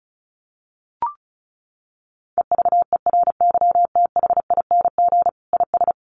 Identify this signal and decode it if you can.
Morse code, heard as a keyed tone, about 34 wpm, 700 Hz. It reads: EVERYTHING IS
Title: Sentences from Top 1400 Words 35wpm